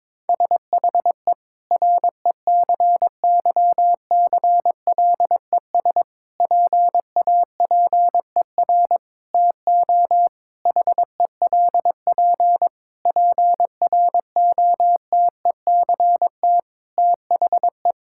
Morse code, heard as a keyed tone, about 22 wpm, 700 Hz. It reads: SHE RECYCLES PAPER TO HELP PROTECT THE
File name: Difficult Sentences with Hints - 22wpm